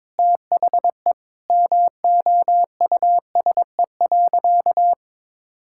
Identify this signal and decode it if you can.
Morse code, heard as a keyed tone, about 21 wpm, 690 Hz. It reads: THE MOUSE.
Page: morse-code-ninja-practice-22wpm